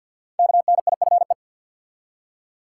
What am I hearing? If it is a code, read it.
Morse code, heard as a keyed tone, about 46 wpm, 700 Hz. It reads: KNIFE
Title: Five Letter Words - 50wpm